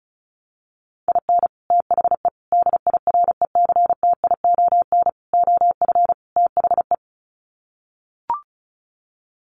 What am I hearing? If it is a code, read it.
Morse code, heard as a keyed tone, about 34 wpm, 700 Hz. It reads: IN THE DIRECTION OF THE
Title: Sets of 5 Words 35wpm